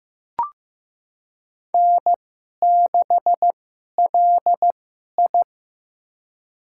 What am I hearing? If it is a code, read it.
Morse code, heard as a keyed tone, about 15 wpm, spaced as for 11 wpm, 700 Hz. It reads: N6LI
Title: USA Call Signs - 4 Characters - 2x Character Spacing - 15wpm